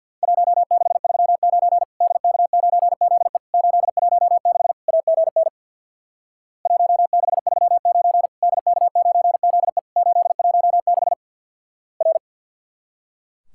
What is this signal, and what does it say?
Morse code, heard as a keyed tone, about 50 wpm, 700 Hz. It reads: 1639 DK9ZE 816 AGN 1639 DK9ZE 816 R
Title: Worked All Europe DX Contest - QTC Exchange with Repeats - 50wpm